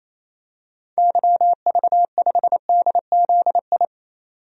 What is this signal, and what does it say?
Morse code, heard as a keyed tone, about 28 wpm, 700 Hz. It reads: YV5DZI
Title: DXCC-100 Call Signs - 6 Characters - 28wpm